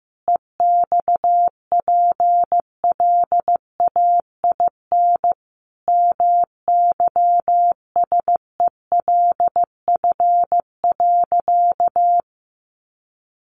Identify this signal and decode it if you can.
Morse code, heard as a keyed tone, about 15 wpm, 700 Hz. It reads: EXPLAIN MYSELF.
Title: Sentences from Top 1300 Words 15wpm